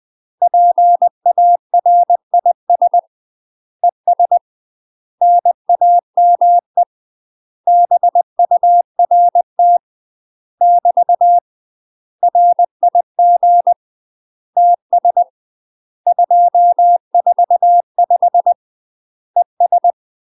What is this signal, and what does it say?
Morse code, heard as a keyed tone, about 20 wpm, 700 Hz. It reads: PARIS ES NAME BURT <BT> RIG TS 245 ES
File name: UK MASTER QSOs TRAINING / 20WPM / UK MASTER QSOs 0029